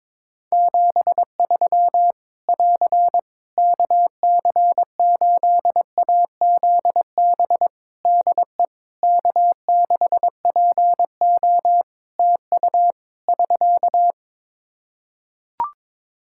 Morse code, heard as a keyed tone, about 22 wpm, 700 Hz. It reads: 73 <AR> KC8AZB DE K6PO TU <SK>
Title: Practice QSO - Exchange 4 - Ending - 22wpm